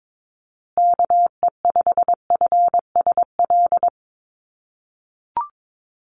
Morse code, heard as a keyed tone, about 22 wpm, 700 Hz. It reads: KE5FSL
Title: USA Call Signs Repeated 3x 22wpm